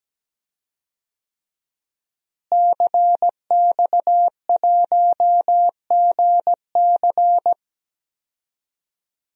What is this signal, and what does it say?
Morse code, heard as a keyed tone, about 17 wpm, 700 Hz. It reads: CX1GC
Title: DXCC-100 Call Signs - 5 Characters - Repeated 3x - 17wpm